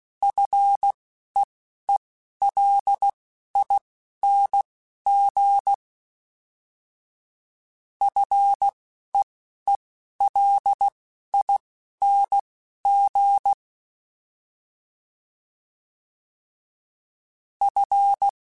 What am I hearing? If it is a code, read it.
Morse code, heard as a keyed tone, about 16 wpm, spaced as for 11 wpm, 800 Hz. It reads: FEELING FEELING F
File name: Fs / Single F mp3s / Single F Prefixes 3x2 Twice @10